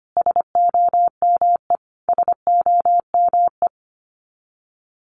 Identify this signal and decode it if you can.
Morse code, heard as a keyed tone, about 25 wpm, 700 Hz. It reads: SOME SOME
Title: Top 500 Words Repeated 25wpm